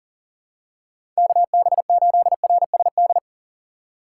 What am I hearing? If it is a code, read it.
Morse code, heard as a keyed tone, about 39 wpm, 700 Hz. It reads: KB8RSD
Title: USA Call Signs - 6 Characters - 2x Character Spacing - 40wpm